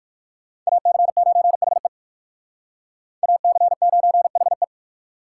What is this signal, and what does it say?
Morse code, heard as a keyed tone, about 57 wpm, 690 Hz. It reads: AC9HE AC9HE